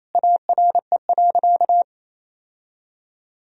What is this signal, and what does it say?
Morse code, heard as a keyed tone, about 28 wpm, 700 Hz. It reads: ARE.
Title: Sentences from Top 1000 Words - Review Part 1 - 28wpm